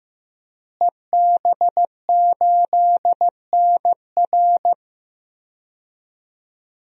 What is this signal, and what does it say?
Morse code, heard as a keyed tone, about 15 wpm, 700 Hz. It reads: EB8NR
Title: DXCC-100 Call Signs - 5 Characters - Repeated 2x - 15wpm